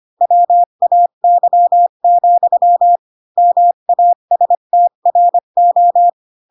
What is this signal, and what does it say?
Morse code, heard as a keyed tone, about 25 wpm, 700 Hz. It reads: WAY, MASTRO